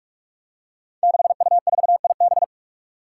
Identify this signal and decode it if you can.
Morse code, heard as a keyed tone, about 44 wpm, 700 Hz. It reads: 6U4IB
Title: Five Challenging Letter-Number Combinations 45wpm